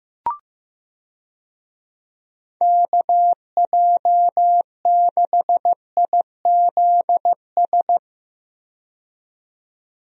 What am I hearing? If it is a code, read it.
Morse code, heard as a keyed tone, about 15 wpm, 700 Hz. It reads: KJ6IZS
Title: USA Call Signs - 6 Characters - 15wpm